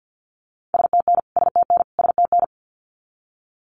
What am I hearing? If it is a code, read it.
Morse code, heard as a keyed tone, about 50 wpm, 700 Hz. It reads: STN STN STN
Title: Abbreviations Repeated 3x - 50wpm